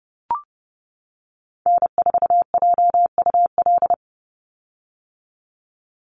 Morse code, heard as a keyed tone, about 30 wpm, 700 Hz. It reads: N4JUL
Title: USA Call Signs 30wpm